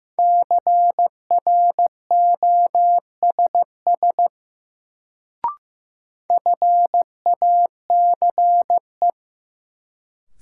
Morse code, heard as a keyed tone, about 15 wpm, 700 Hz. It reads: CROSS FACE
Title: Sentences from Top 300 Words - New Word Drill - 15wpm